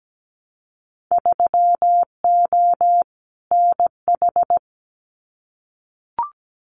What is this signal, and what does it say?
Morse code, heard as a keyed tone, about 17 wpm, 700 Hz. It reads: 3O NH
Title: Winter Field Day Exchanges Encore 17wpm